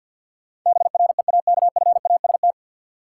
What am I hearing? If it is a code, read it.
Morse code, heard as a keyed tone, about 52 wpm, 700 Hz. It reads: BREAKFAST
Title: Sentences from Top 1500 Words - New Word Drill - 50wpm